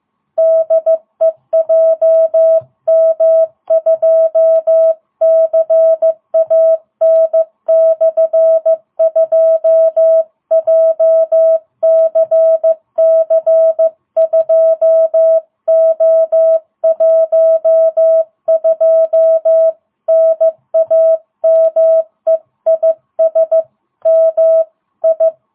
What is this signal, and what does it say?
Morse code, heard as a keyed tone, about 15 wpm, 660 Hz. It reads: DE JM2CAN/2 JCC2O12 NAME IS MI